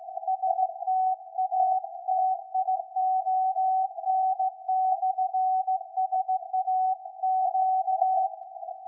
Morse code, heard as a keyed tone, about 16 wpm, 750 Hz. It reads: STATION/SAQ
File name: iz7slz / VLF / SAQ 13-02-2023